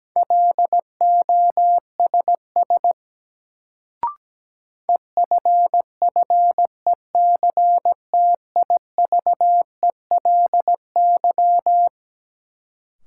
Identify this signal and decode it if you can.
Morse code, heard as a keyed tone, about 17 wpm, 700 Hz. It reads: LOSS EFFECTIVELY
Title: Sentences from Top 1400 Words - New Word Drill Alt - 17wpm